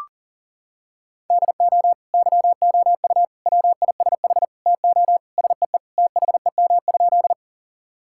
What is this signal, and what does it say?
Morse code, heard as a keyed tone, about 40 wpm, 700 Hz. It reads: DO YOU WISH TO SEE THEM?
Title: Sentences from Top 1600 Words 40wpm